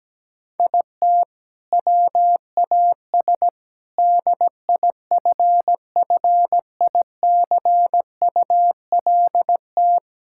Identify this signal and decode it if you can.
Morse code, heard as a keyed tone, about 17 wpm, 700 Hz. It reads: IT WAS DIFFICULT
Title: Sentences from Top 1000 Words - Review Part 6 - 17wpm